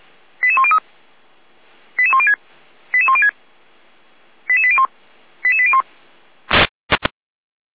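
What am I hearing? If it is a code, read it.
Five-tone selective call (ZVEI), heard as quick runs of five tones, tones of about 80 ms. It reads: 88103 88106 88106 88801 88801
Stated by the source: ZVEI five-tone call received by SDR.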